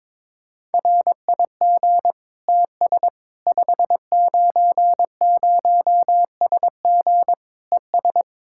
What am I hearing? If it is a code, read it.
Morse code, heard as a keyed tone, about 22 wpm, 700 Hz. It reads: RIG TS 590SG ES
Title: K1USN SST QSO Practice 22wpm